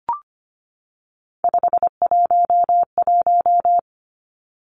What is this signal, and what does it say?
Morse code, heard as a keyed tone, about 25 wpm, 700 Hz. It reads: 511
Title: Three Challenging Letter-Number Combinations 25wpm